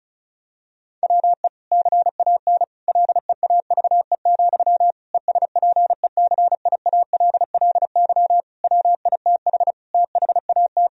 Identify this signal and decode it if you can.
Morse code, heard as a keyed tone, about 34 wpm, 700 Hz. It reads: WE CAN LEAVE, ESPECIALLY WITH THAT
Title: Sentences from Top 1000 Words - Review Part 4 - 35wpm